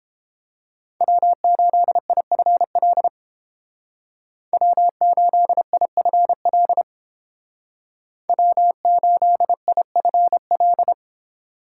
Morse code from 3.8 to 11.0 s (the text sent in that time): W8IFL W8IFL